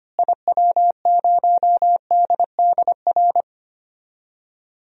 Morse code, heard as a keyed tone, about 25 wpm, 700 Hz. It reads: IW0DDR
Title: DXCC-100 Call Signs - 6 Characters - Repeated 3x - 25wpm